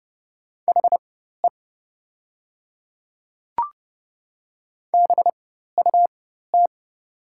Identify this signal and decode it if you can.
Morse code, heard as a keyed tone, about 30 wpm, spaced as for 14 wpm, 700 Hz. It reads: HE BUT
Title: 4x Character Spacing - Top 100 Words - 30wpm